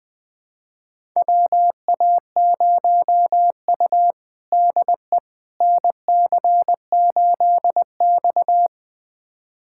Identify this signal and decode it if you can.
Morse code, heard as a keyed tone, about 20 wpm, 700 Hz. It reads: WA0U DE NC8X
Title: USA Callsign DE Callsign - 4 Characters - 20wpm